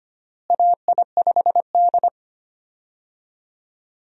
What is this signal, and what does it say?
Morse code, heard as a keyed tone, about 25 wpm, 700 Hz. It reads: AI5D
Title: USA Call Signs - 4 Characters - Repeated 3x - 25wpm